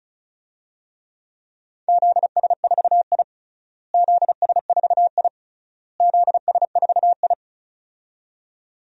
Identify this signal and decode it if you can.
Morse code, heard as a keyed tone, about 36 wpm, 700 Hz. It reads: ZS4I ZS4I ZS4I
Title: DXCC-100 Call Signs - 4 Characters - Repeated 3x - 35wpm